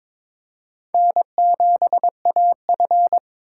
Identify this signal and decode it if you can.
Morse code, heard as a keyed tone, about 22 wpm, 700 Hz. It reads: N7AF
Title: USA Call Signs - 4 Characters - 3x Character Spacing - 22wpm